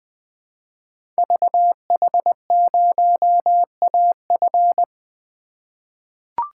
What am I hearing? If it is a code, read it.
Morse code, heard as a keyed tone, about 20 wpm, 700 Hz. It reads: VH0AF
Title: Code Groups - 5 Letter:Numbers - 20wpm